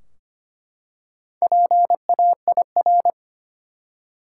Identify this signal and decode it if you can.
Morse code, heard as a keyed tone, about 25 wpm, 700 Hz. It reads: PAIR
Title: Four Letter Words - 25wpm